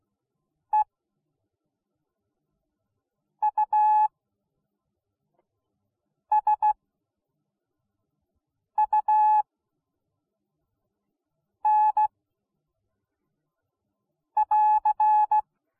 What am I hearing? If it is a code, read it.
Morse code, heard as a keyed tone, about 16 wpm, spaced as for 4 wpm, 850 Hz. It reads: EUSUN <AR>